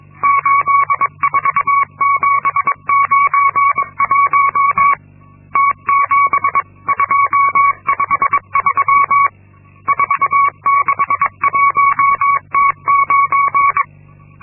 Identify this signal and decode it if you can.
Morse code, heard as a keyed tone, about 22 wpm, 1120 Hz. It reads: 84791 T7253 461T9
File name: Linksmakalnis-2